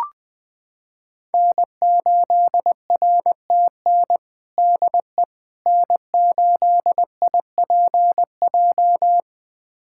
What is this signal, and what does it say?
Morse code, heard as a keyed tone, about 20 wpm, 700 Hz. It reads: N8RTN DE N8IPJ
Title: USA Callsign DE Callsign - 5 Characters - 20wpm